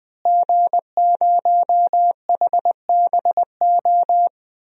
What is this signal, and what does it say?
Morse code, heard as a keyed tone, about 21 wpm, 700 Hz. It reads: G0HBO
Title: DXCC-100 Call Signs - 5 Characters - Repeated 3x - 20wpm